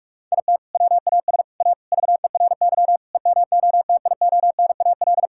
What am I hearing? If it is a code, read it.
Morse code, heard as a keyed tone, about 41 wpm, 700 Hz. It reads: IT WAS A VERY EMOTIONAL
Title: Sentences from Top 1000 Words - Review Part 5 - 45wpm